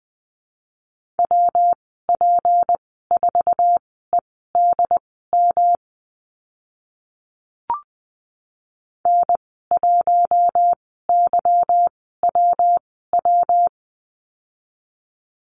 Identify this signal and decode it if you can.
Morse code, heard as a keyed tone, about 20 wpm, spaced as for 15 wpm, 700 Hz. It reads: WP4EDM N1YWW
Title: 186 Lesson 30g - Letter b - Call Signs 20wpm